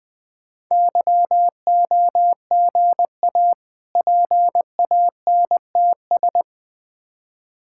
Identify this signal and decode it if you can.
Morse code, heard as a keyed tone, about 20 wpm, 700 Hz. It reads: YOGA PANTS
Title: Difficult Words - Speed-Racing with Hints - 20wpm